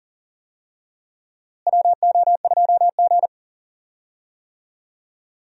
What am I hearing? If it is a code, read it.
Morse code, heard as a keyed tone, about 38 wpm, 700 Hz. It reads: WO2G
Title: USA Call Signs - 4 Characters - Repeated 2x - 40wpm